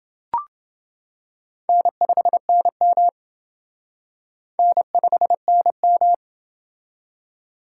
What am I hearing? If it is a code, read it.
Morse code, heard as a keyed tone, about 28 wpm, 700 Hz. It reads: N5NM N5NM